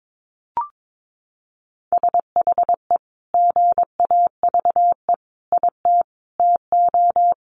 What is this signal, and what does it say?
Morse code, heard as a keyed tone, about 22 wpm, 700 Hz. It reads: SHE GAVE IT TO
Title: Sentences from Top 1500 Words 22wpm